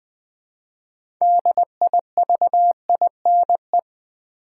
Morse code, heard as a keyed tone, about 20 wpm, 700 Hz. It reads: DIVINE
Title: 6 Letter Words - First 3 Letters Given - 20wpm